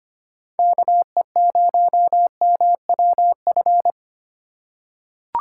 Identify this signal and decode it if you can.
Morse code, heard as a keyed tone, about 25 wpm, 700 Hz. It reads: KE0MWF
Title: USA Call Signs - 6x Character Spacing - 25wpm